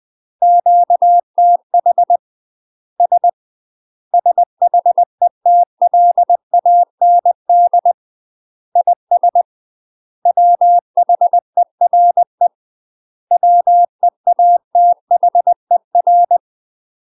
Morse code, heard as a keyed tone, about 20 wpm, 700 Hz. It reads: QTH S SHETLAND IS WHERE WEATHER